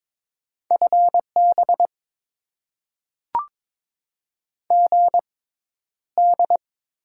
Morse code, heard as a keyed tone, about 22 wpm, spaced as for 11 wpm, 700 Hz. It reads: FB GD
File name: Top 90 QSO Elements - 6x Character Spacing - 22wpm